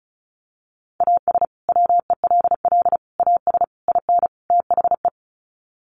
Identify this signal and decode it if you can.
Morse code, heard as a keyed tone, about 34 wpm, 700 Hz. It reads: AS WELL AS IN THE
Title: Sets of 5 Words Encore 35wpm